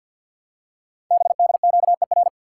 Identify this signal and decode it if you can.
Morse code, heard as a keyed tone, about 48 wpm, 700 Hz. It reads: BDQER
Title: Code Groups - 5 Letters - 50wpm